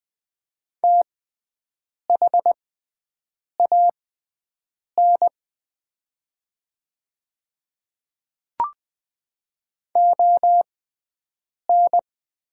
Morse code, heard as a keyed tone, about 20 wpm, spaced as for 7 wpm, 700 Hz. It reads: THAN ON
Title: 6x Character Spacing - Top 200 Words - 20wpm